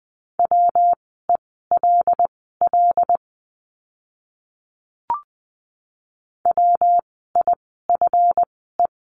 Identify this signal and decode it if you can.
Morse code, heard as a keyed tone, about 20 wpm, spaced as for 15 wpm, 700 Hz. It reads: WELL WIFE
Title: 139 Lesson 23f - Letter f - Words 20wpm